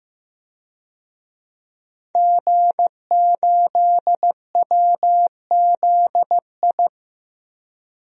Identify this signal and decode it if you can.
Morse code, heard as a keyed tone, about 15 wpm, 700 Hz. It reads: G8WZI
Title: DXCC-100 Call Signs - 5 Characters - 15wpm